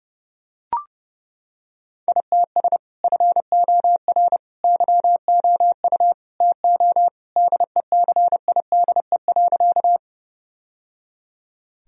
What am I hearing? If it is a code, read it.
Morse code, heard as a keyed tone, about 30 wpm, 700 Hz. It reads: ITS FOR YOU TO DECIDE.
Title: Sentences from Top 1000 Words - Review Part 3 - 30wpm